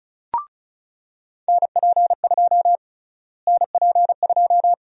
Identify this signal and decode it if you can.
Morse code, heard as a keyed tone, about 34 wpm, 700 Hz. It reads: NP2 NP2
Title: USA Call Sign Prefixes - Repeated 2x - 35wpm